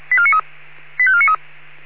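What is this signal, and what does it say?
Five-tone selective call (ZVEI), heard as quick runs of five tones, tones of about 70 ms. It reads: 85493
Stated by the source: ZVEI five-tone call 85492 received by SDR.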